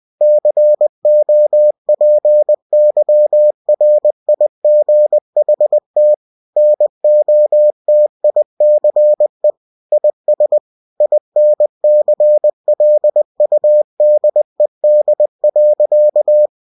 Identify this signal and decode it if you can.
Morse code, heard as a keyed tone, about 20 wpm, 600 Hz. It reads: COPYRIGHT NOTICE IS INCLUDED.
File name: cw-books / winnie-the-pooh / winnie-20wpm-page-0122